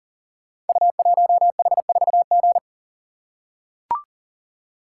Morse code, heard as a keyed tone, about 38 wpm, 690 Hz. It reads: U1H4G
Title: Code Groups - 5 Letter:Numbers - 40wpm